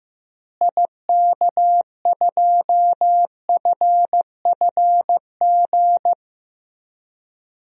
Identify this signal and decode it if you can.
Morse code, heard as a keyed tone, about 15 wpm, 700 Hz. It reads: IK2FFG
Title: DXCC-100 Call Signs - 6 Characters - Repeated 2x - 15wpm